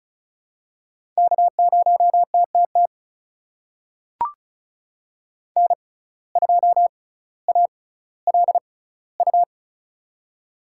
Morse code, heard as a keyed tone, about 36 wpm, spaced as for 18 wpm, 700 Hz. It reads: K0TTT N2ALU
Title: USA Call Signs - 5 Characters - 6x Character Spacing - 35wpm